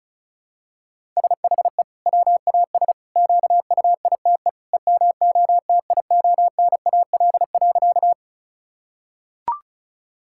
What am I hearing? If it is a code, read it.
Morse code, heard as a keyed tone, about 36 wpm, 700 Hz. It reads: SHE WAS QUITE EMOTIONAL.
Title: Sentences from Top 1000 Words - Review Part 5 - 35wpm